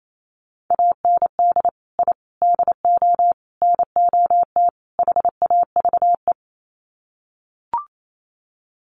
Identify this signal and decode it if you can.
Morse code, heard as a keyed tone, about 28 wpm, 700 Hz. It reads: AND I DO NOT HAVE
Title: Sets of 5 Words Encore 28wpm